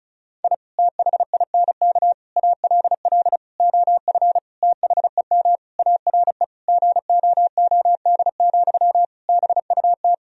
Morse code, heard as a keyed tone, about 34 wpm, 700 Hz. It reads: I THINK ALL OF THEM ARE GOOD, BUT
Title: Sentences from Top 1000 Words - Review Part 4 - 35wpm